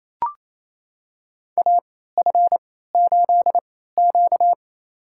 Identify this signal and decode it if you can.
Morse code, heard as a keyed tone, about 28 wpm, spaced as for 16 wpm, 700 Hz. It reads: AF8Q
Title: USA Call Signs - 3x Character Spacing - 28wpm